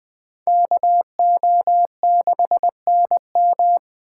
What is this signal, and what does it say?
Morse code, heard as a keyed tone, about 20 wpm, 700 Hz. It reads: KO6NM
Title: Speed Racing - USA 2x2 Call Signs 17wpm